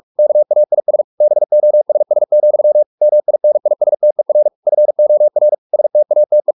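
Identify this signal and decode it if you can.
Morse code, heard as a keyed tone, about 44 wpm, 600 Hz. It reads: KAIS BOSS, MINISTER FOR STATE
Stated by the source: Morse practice at 45 wpm, the novel Never by Ken Follett in English, from CW med Gnister.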